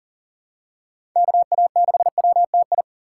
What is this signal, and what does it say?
Morse code, heard as a keyed tone, about 40 wpm, 700 Hz. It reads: KA6WTI
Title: USA Call Signs - 6x Character Spacing - 40wpm